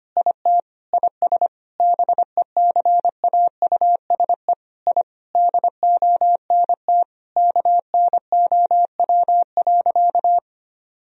Morse code, heard as a keyed tone, about 25 wpm, 700 Hz. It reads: IT IS BECAUSE I DONT KNOW.
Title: Sentences from Top 1000 Words - Review Part 3 - 25wpm